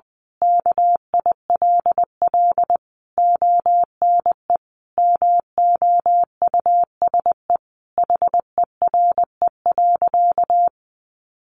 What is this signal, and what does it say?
Morse code, heard as a keyed tone, about 20 wpm, 700 Hz. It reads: KILL ONE MOUSE HERE.
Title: A Sound Of Thunder 20wpm